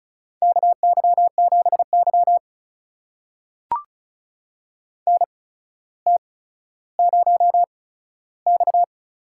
Morse code, heard as a keyed tone, about 36 wpm, spaced as for 15 wpm, 700 Hz. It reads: KY7Y NT0X
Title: USA Call Signs - 8x Character Spacing - 35wpm